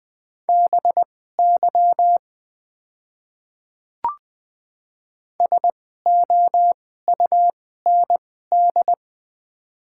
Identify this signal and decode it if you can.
Morse code, heard as a keyed tone, about 20 wpm, spaced as for 14 wpm, 700 Hz. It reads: BY SOUND
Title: 2x Character Spacing - Top 300 Words - 20wpm